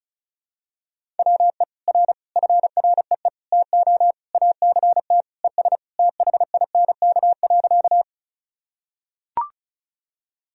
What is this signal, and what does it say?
Morse code, heard as a keyed tone, about 36 wpm, 700 Hz. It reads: WE R FREE TO ACT ES THINK.
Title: Sentences from Top 500 Words and Abbreviations 35wpm